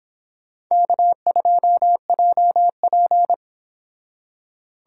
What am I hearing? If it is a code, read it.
Morse code, heard as a keyed tone, about 24 wpm, 700 Hz. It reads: K2JP